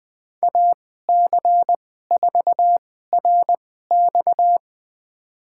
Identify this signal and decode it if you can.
Morse code, heard as a keyed tone, about 20 wpm, spaced as for 14 wpm, 700 Hz. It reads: AC4RX
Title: USA Call Signs - 5 Characters - 2x Character Spacing - 20wpm